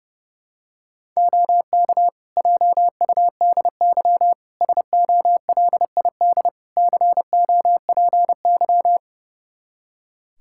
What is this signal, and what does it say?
Morse code, heard as a keyed tone, about 31 wpm, 700 Hz. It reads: OK JUDY SOLID COPY
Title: Practice QSO - Exchange 3 - More Info - 30wpm